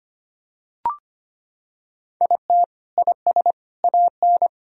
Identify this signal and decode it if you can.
Morse code, heard as a keyed tone, about 25 wpm, 700 Hz. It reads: IT IS AN
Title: Sentences from Top 1000 Words - Review Part 4 - 25wpm